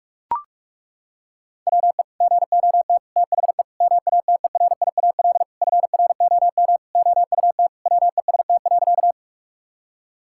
Morse code, heard as a keyed tone, about 43 wpm, 700 Hz. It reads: WE GOT THE MATERIAL FROM OUT WEST.
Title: Sentences from Top 1000 Words - Review Part 5 - 45wpm